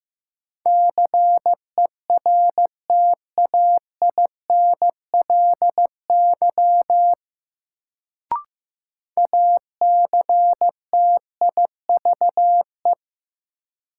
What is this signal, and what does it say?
Morse code, heard as a keyed tone, about 16 wpm, 700 Hz. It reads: CERTAINLY ACTIVE
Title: Sentences from Top 900 Words - New Word Drill - 15wpm